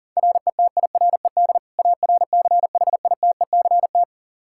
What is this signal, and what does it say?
Morse code, heard as a keyed tone, about 40 wpm, 700 Hz. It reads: RETIRED ARCHITECT
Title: Practice QSO - Exchange 3 - More Info - 40wpm